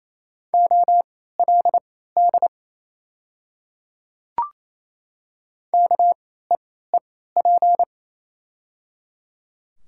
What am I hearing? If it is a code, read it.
Morse code, heard as a keyed tone, about 28 wpm, spaced as for 16 wpm, 690 Hz. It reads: OLD KEEP
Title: 3x Character Spacing - Top 500 Words - 28wpm